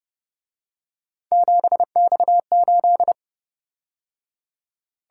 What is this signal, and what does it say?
Morse code, heard as a keyed tone, about 29 wpm, 700 Hz. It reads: 7X8